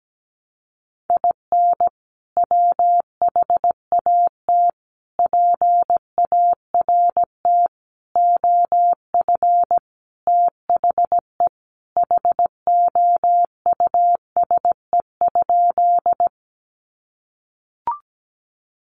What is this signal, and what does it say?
Morse code, heard as a keyed tone, about 17 wpm, 700 Hz. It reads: IN WHAT PART OF THE HOUSE?
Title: Sentences from Top 200 Words 17wpm